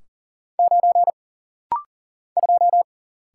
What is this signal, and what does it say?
Morse code, heard as a keyed tone, about 40 wpm, 700 Hz. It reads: 9 2
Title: Single Letter-Number - Rapid-Fire 40wpm